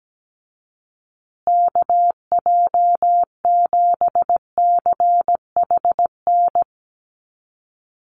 Morse code, heard as a keyed tone, about 17 wpm, 700 Hz. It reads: KJ7CHN
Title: USA Call Signs Repeated 3x 17wpm